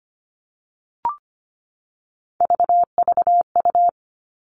Morse code, heard as a keyed tone, about 25 wpm, 700 Hz. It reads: VVU
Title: All Cut Numbers - 3-Digit 25wpm